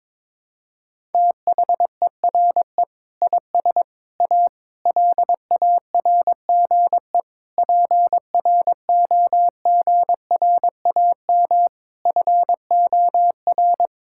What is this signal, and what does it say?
Morse code, heard as a keyed tone, about 22 wpm, 700 Hz. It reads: THERE IS A LARGE PROGRAM FOR
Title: Sentences from Top 1000 Words - Review Part 5 - 22wpm